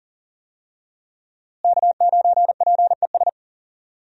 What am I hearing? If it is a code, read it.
Morse code, heard as a keyed tone, about 40 wpm, 700 Hz. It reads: K9PES
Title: USA Call Signs - 5 Characters - Repeated 2x - 40wpm